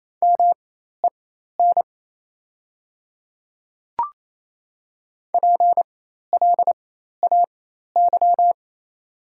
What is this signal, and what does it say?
Morse code, heard as a keyed tone, about 28 wpm, spaced as for 13 wpm, 700 Hz. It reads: MEN PLAY
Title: 4x Character Spacing - Top 200 Words - 28wpm